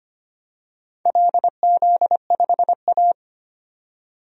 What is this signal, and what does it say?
Morse code, heard as a keyed tone, about 25 wpm, 700 Hz. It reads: LZ5A
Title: DXCC-100 Call Signs - 4 Characters - 25wpm